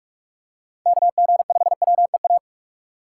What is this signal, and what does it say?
Morse code, heard as a keyed tone, about 46 wpm, 700 Hz. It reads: KG5WEA
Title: USA Call Signs - 6 Characters - 2x Character Spacing - 45wpm